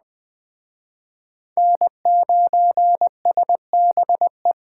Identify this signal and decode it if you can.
Morse code, heard as a keyed tone, about 23 wpm, 700 Hz. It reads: N9SBE
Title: Speed Racing - USA 1x3 Call Signs 20wpm